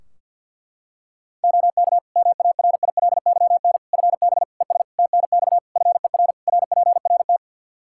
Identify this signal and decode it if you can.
Morse code, heard as a keyed tone, about 48 wpm, 700 Hz. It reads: OK MARILYN FB ES TNX FER RPRT
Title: Practice QSO - Exchange 2 - Info - 50wpm